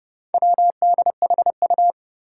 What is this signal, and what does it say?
Morse code, heard as a keyed tone, about 32 wpm, 700 Hz. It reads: WDHU
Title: Four Challenging Letter-Number Combinations 30wpm